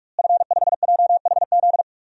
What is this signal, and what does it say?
Morse code, heard as a keyed tone, about 46 wpm, 700 Hz. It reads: F5JHZ
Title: DXCC-100 Call Signs - 5 Characters - Repeated 2x - 45wpm